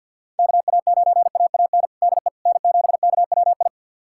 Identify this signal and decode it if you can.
Morse code, heard as a keyed tone, about 48 wpm, 690 Hz. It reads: KA9AAN DE N7KWI
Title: USA Callsign DE Callsign 50wpm